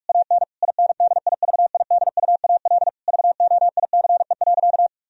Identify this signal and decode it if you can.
Morse code, heard as a keyed tone, about 41 wpm, 700 Hz. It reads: AN INDIVIDUAL VOICE.
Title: Sentences from Top 1000 Words - Review Part 4 - 45wpm